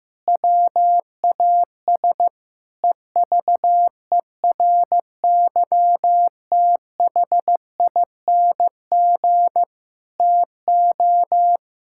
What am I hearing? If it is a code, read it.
Morse code, heard as a keyed tone, about 15 wpm, 700 Hz. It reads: WAS EVERYTHING TO
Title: Sentences from Top 1600 Words 15wpm